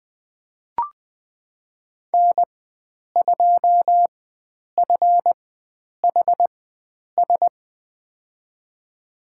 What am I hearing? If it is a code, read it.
Morse code, heard as a keyed tone, about 21 wpm, spaced as for 9 wpm, 700 Hz. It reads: N2FHS
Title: USA Call Signs - 5 Characters - 4x Character Spacing - 20wpm